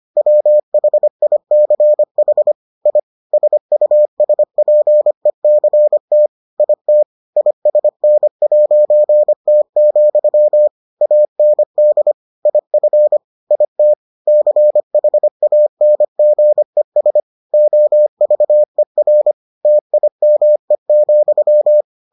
Morse code, heard as a keyed tone, about 25 wpm, 600 Hz. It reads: WHICH I SUSPECT IT ISN'T, AND IF IT CHANGES OVER TIME,